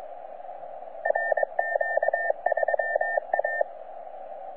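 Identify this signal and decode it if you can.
Morse code, heard as a keyed tone, about 22 wpm, 1800 Hz, with radio noise. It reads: RQ3A
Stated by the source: C91RF DXpedition station heard on 10 m CW, from DX World.net.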